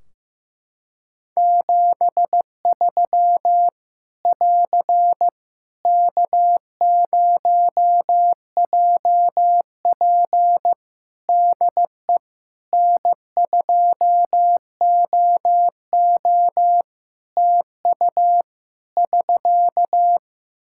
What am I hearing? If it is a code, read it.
Morse code, heard as a keyed tone, about 15 wpm, 700 Hz. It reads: 73 <AR> K0JP DE N2OO TU <SK>
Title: Practice QSO - Exchange 4 - Ending - 15wpm